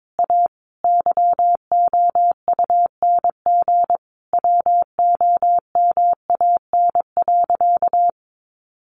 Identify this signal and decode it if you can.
Morse code, heard as a keyed tone, about 22 wpm, 700 Hz. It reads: A YOUNG WOMAN.
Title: Sentences from Top 1400 Words 22wpm